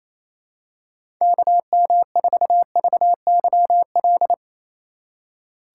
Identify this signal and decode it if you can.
Morse code, heard as a keyed tone, about 28 wpm, 700 Hz. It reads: KM4VYL
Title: USA Call Signs - 6 Characters - 2x Character Spacing - 28wpm